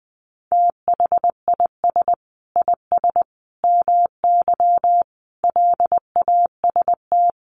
Morse code, heard as a keyed tone, about 20 wpm, 700 Hz. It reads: THIS IS MY LAST
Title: Sentences from Top 1300 Words 20wpm